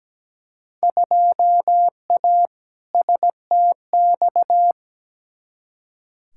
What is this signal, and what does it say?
Morse code, heard as a keyed tone, about 17 wpm, 700 Hz. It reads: 2A STX
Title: Field Day Exchanges - 17wpm